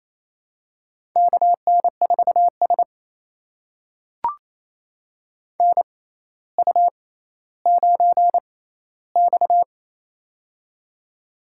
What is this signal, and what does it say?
Morse code, heard as a keyed tone, about 28 wpm, spaced as for 14 wpm, 700 Hz. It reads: KN4S NU9X
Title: USA Call Signs - 4 Characters - 6x Character Spacing - 28wpm